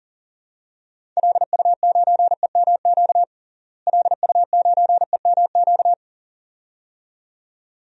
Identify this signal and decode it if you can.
Morse code, heard as a keyed tone, about 37 wpm, 700 Hz. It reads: LU9EMQ LU9EMQ
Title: DXCC-100 Call Signs - 6 Characters - Repeated 2x - 40wpm